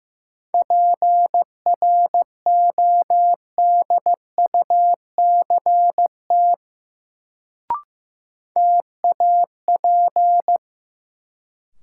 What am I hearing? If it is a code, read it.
Morse code, heard as a keyed tone, about 15 wpm, 700 Hz. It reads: PRODUCT TAP
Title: Sentences from Top 1400 Words - New Word Drill Alt - 15wpm